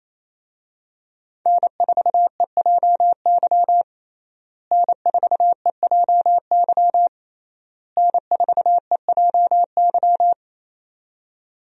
Morse code, heard as a keyed tone, about 28 wpm, 700 Hz. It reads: N4EJY N4EJY N4EJY
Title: USA Call Signs - 5 Characters - Repeated 3x - 28wpm